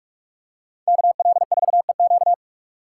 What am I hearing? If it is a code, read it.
Morse code, heard as a keyed tone, about 44 wpm, 700 Hz. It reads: KL4EQ